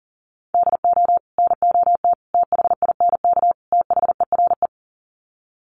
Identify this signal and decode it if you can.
Morse code, heard as a keyed tone, about 40 wpm, 700 Hz. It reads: DO NOT THINK THERE
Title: Sets of 4 Words 40wpm